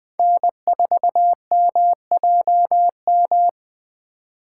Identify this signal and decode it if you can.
Morse code, heard as a keyed tone, about 20 wpm, 700 Hz. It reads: N4MJM
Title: USA Call Signs - 2x Character Spacing - 20wpm